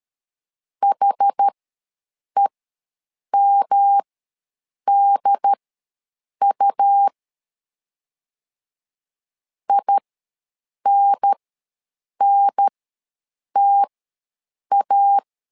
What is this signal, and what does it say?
Morse code, heard as a keyed tone, about 12 wpm, spaced as for 7 wpm, 800 Hz. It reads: HEMDU INNTA
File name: LEZIONI / 3 MWRO / 3 M ws 12wpm 1745